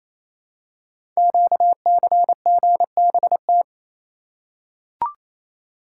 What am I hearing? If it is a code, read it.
Morse code, heard as a keyed tone, about 27 wpm, 700 Hz. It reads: QCGBT
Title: Code Groups - 5 Letter:Numbers - 28wpm